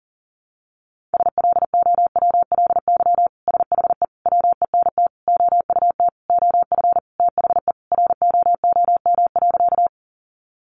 Morse code, heard as a keyed tone, about 37 wpm, 700 Hz. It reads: SLOWLY SHE WENT OUT OF THE ROOM.
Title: Sentences from Top 1100 Words 40wpm